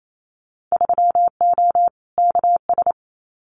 Morse code, heard as a keyed tone, about 28 wpm, 700 Hz. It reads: 3O KS
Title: Winter Field Day Exchanges 28wpm